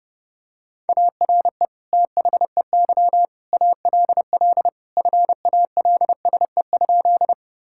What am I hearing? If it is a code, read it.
Morse code, heard as a keyed tone, about 31 wpm, 700 Hz. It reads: ARE THEY ALL FALSE?
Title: Sentences from Top 1600 Words 30wpm